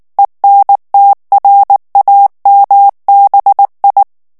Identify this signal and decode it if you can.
Morse code, heard as a keyed tone, about 19 wpm, 800 Hz. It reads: ENTRAMBI
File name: morse mp3 / ITALIEN texte clair 2 vitesse 18 mots minute